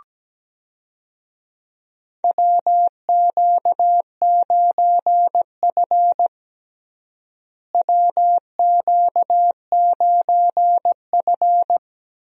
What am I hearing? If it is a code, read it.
Morse code, heard as a keyed tone, about 17 wpm, 700 Hz. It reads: WQ9F WQ9F
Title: USA Call Signs - 4 Characters - Repeated 2x - 17wpm